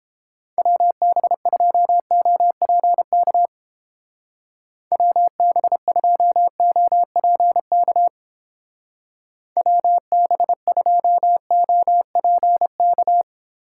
Morse code, heard as a keyed tone, about 31 wpm, 700 Hz. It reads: WB2OPK WB2OPK WB2OPK